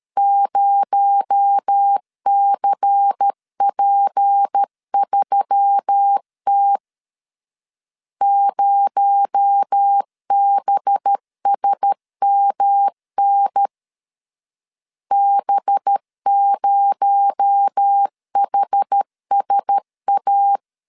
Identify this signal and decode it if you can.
Morse code, heard as a keyed tone, about 13 wpm, 800 Hz. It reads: 0CP3T 0BSMN B0HSA
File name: LEZIONI / 6 PKX0 / 6 PKX0 12wpm 2231